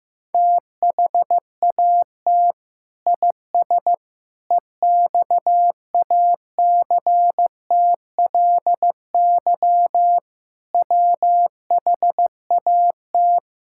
Morse code, heard as a keyed tone, about 15 wpm, 700 Hz. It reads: THAT IS EXACTLY WHAT
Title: Sentences from Top 1000 Words - Review Part 6 - 15wpm